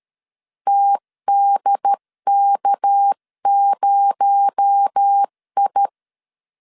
Morse code, heard as a keyed tone, about 13 wpm, 800 Hz. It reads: TDK0I
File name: CW / DET / 1 6 1500C S C 12wpm 3533